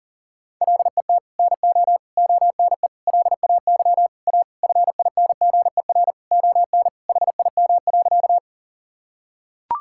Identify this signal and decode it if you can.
Morse code, heard as a keyed tone, about 40 wpm, 700 Hz. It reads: LET NO ONE LAY A FINGER ON HIM.
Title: Sentences from Top 1600 Words 40wpm